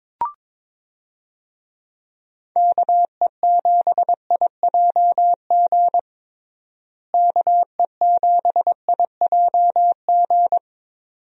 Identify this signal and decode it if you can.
Morse code, heard as a keyed tone, about 22 wpm, 700 Hz. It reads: KE7IJG KE7IJG
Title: USA Call Signs - 6 Characters - Repeated 2x - 22wpm